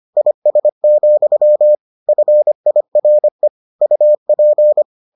Morse code, heard as a keyed tone, about 25 wpm, 600 Hz. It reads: IS, FIRE UP